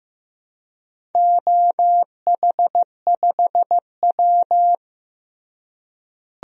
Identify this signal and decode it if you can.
Morse code, heard as a keyed tone, about 15 wpm, 700 Hz. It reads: OH5W
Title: DXCC-100 Call Signs - 4 Characters - Repeated 3x - 15wpm